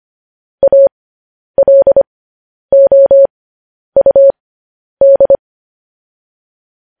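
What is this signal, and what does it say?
Morse code, heard as a keyed tone, about 25 wpm, spaced as for 10 wpm, 550 Hz. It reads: ALOUD
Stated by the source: Morse code news headlines broadcast, opening with vvv.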